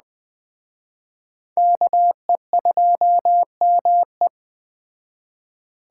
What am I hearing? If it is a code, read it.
Morse code, heard as a keyed tone, about 20 wpm, 700 Hz. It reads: KE2ME